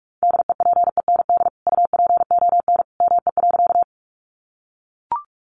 Morse code, heard as a keyed tone, about 44 wpm, 700 Hz. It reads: DEPEND UPON ME.
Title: Sentences from Top 1300 Words 45wpm